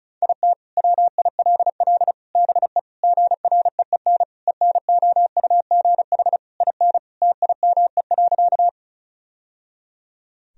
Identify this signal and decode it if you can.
Morse code, heard as a keyed tone, about 36 wpm, 700 Hz. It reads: IT WILL BE GREEN ENOUGH IN TIME.
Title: Sentences from Top 1000 Words - Review Part 2 - 35wpm